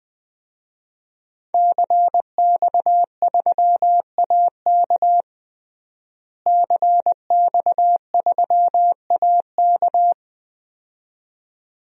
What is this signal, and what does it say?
Morse code, heard as a keyed tone, about 21 wpm, 700 Hz. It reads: CX3AK CX3AK
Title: DXCC-100 Call Signs - 5 Characters - Repeated 2x - 20wpm